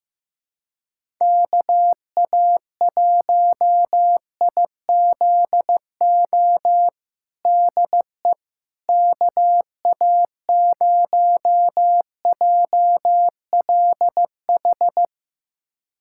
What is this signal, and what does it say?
Morse code, heard as a keyed tone, about 15 wpm, 700 Hz. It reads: KA1IZO DE KA0JLH
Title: USA Callsign DE Callsign - 6 Characters - 15wpm